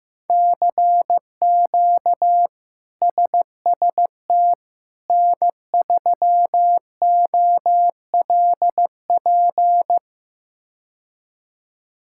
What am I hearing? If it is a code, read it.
Morse code, heard as a keyed tone, about 15 wpm, 700 Hz. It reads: CQ SST N3OLP
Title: K1USN SST QSO Practice - 15wpm